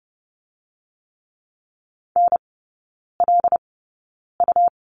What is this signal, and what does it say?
Morse code, heard as a keyed tone, about 30 wpm, 700 Hz. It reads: N L U
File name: Lesson 19e - Letter w - Three Characters 30wpm